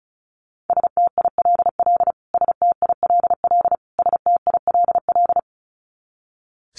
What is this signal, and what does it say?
Morse code, heard as a keyed tone, about 34 wpm, 700 Hz. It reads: STILL STILL STILL
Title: Top 500 Words Repeated 35wpm